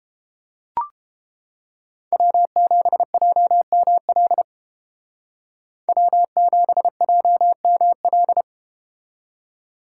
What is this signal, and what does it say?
Morse code, heard as a keyed tone, about 33 wpm, 700 Hz. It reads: W7JML W7JML